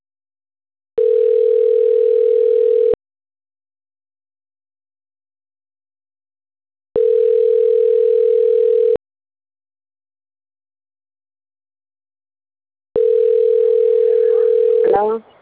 Ringback tone (North America), 440 + 480 Hz.